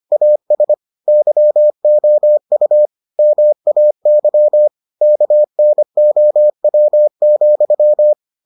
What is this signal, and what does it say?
Morse code, heard as a keyed tone, about 25 wpm, 600 Hz. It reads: AS YOU MAY KNOW,